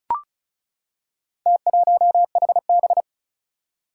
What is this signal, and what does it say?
Morse code, heard as a keyed tone, about 36 wpm, 700 Hz. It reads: T1HB
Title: Four Challenging Letter-Number Combinations 35wpm